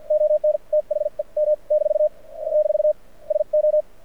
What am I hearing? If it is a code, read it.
Morse code, heard as a keyed tone, about 53 wpm, spaced as for 45 wpm, 600 Hz. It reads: ON THE M - - SO
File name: archived / CRSnet